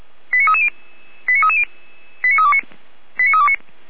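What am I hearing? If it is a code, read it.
Five-tone selective call (ZVEI), heard as quick runs of five tones, tones of about 70 ms. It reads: 89330 89330 89329 89329
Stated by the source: ZVEI five-tone selective call received on an SDR.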